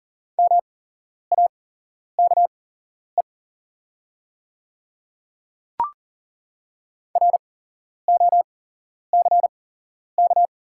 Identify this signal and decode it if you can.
Morse code, heard as a keyed tone, about 40 wpm, spaced as for 11 wpm, 690 Hz. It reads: MAKE ROCK
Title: 8x Character Spacing - Top 500 Words - 40wpm